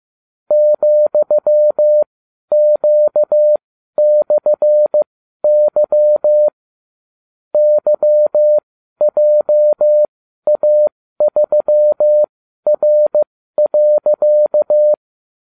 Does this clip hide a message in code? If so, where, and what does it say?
Yes: Morse code from 0.5 to 14.9 s, heard as a keyed tone, about 15 wpm, spaced as for 12 wpm, 600 Hz. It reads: ,Q/Y YJA3R.